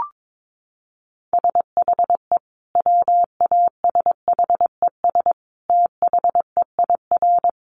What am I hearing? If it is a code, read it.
Morse code, heard as a keyed tone, about 22 wpm, 700 Hz. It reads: SHE WASHES THEIR
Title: Easy Phrases 22wpm